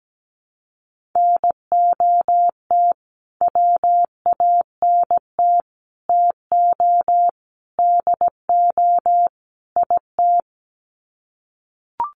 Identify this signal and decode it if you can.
Morse code, heard as a keyed tone, about 17 wpm, 700 Hz. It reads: NOT WANT TO DO IT
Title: Sets of 5 Words Encore 17wpm